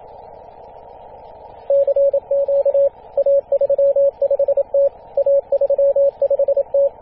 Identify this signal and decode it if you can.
Morse code, heard as a keyed tone, about 26 wpm, 560 Hz, with radio noise. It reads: CQ A35T A35T
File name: posts / 2016-feb-27-0448z-a35t-24-898mhz-cw